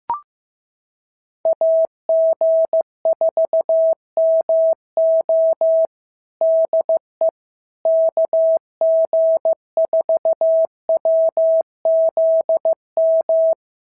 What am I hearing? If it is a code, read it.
Morse code, heard as a keyed tone, about 15 wpm, 650 Hz. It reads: AG4MO DE KG4WZM
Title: SOTA QSO Practice 15wpm